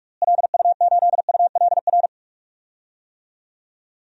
Morse code, heard as a keyed tone, about 46 wpm, 700 Hz. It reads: LU8ULR
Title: DXCC-100 Call Signs - 6 Characters - Repeated 3x - 45wpm